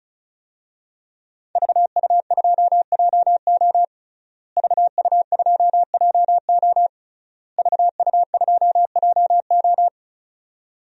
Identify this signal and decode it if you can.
Morse code, heard as a keyed tone, about 36 wpm, 700 Hz. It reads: VU2JO VU2JO VU2JO
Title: DXCC-100 Call Signs - 5 Characters - Repeated 3x - 35wpm